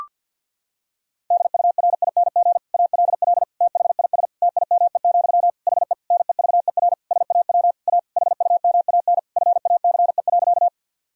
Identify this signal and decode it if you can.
Morse code, heard as a keyed tone, about 50 wpm, 700 Hz. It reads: DURING ALL THIS TIME, HE NEVER SAW A HUMAN FACE.
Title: Sentences from Top 1000 Words - Review Part 3 - 50wpm